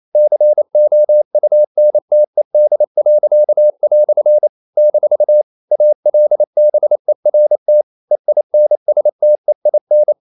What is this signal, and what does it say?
Morse code, heard as a keyed tone, about 28 wpm, 600 Hz. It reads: COUNTED." - ALBERT EINSTEIN
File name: CW / Practice / EnglishQuotations0128